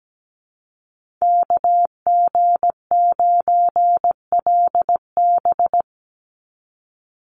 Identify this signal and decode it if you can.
Morse code, heard as a keyed tone, about 17 wpm, 700 Hz. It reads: KG9LB
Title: Call Signs - 17wpm